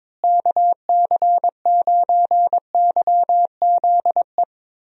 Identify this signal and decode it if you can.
Morse code, heard as a keyed tone, about 22 wpm, 700 Hz. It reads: KC9YZE
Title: USA Call Signs - 6 Characters - Repeated 3x - 22wpm